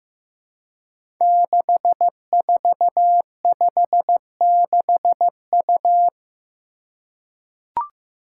Morse code, heard as a keyed tone, about 15 wpm, 700 Hz. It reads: 6456U
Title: Five Challenging Letter-Number Combinations 15wpm